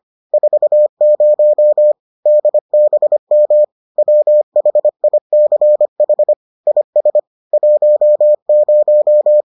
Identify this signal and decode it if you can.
Morse code, heard as a keyed tone, about 25 wpm, 600 Hz. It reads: 40 DBM WHICH IS 10